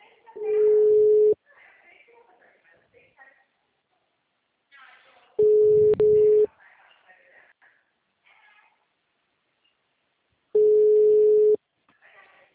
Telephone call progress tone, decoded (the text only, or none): ringback tone